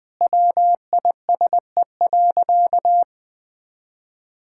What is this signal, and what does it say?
Morse code, heard as a keyed tone, about 21 wpm, 700 Hz. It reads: WISE.
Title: Sentences from Top 1600 Words 20wpm